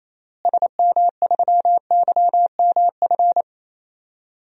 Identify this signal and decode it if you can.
Morse code, heard as a keyed tone, about 28 wpm, 700 Hz. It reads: SM3YMF
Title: DXCC-100 Call Signs - 6 Characters - Repeated 2x - 28wpm